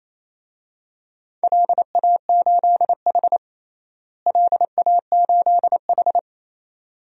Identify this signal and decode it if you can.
Morse code, heard as a keyed tone, about 28 wpm, 700 Hz. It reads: LA8H LA8H
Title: DXCC-100 Call Signs - 4 Characters - Repeated 2x - 28wpm